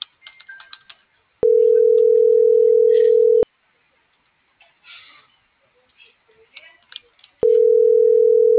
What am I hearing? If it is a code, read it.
Ringback tone (North America), 440 + 480 Hz.
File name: CENTRAL GENERAL / out michelle.carrera